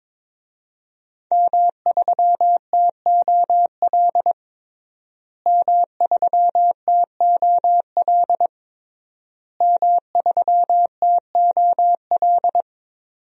Morse code, heard as a keyed tone, about 22 wpm, 700 Hz. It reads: M3TOL M3TOL M3TOL
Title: DXCC-100 Call Signs - 5 Characters - Repeated 3x - 22wpm